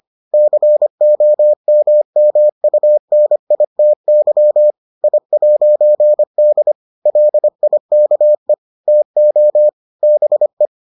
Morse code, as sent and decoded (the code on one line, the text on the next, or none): -.-. --- -- -- ..- -. .. - -.-- / .. .----. -.. / .-.. .. -.- . / - --- / -... .
COMMUNITY I'D LIKE TO BE